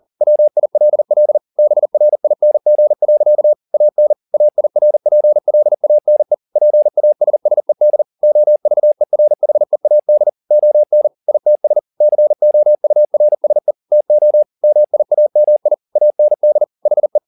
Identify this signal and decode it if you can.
Morse code, heard as a keyed tone, about 40 wpm, 600 Hz. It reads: WILL BRING. AN AIRPLANE PASSED OVERHEAD ON ITS COURSE TO MIAMI AND HE
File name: the old man and the sea 2 40wpm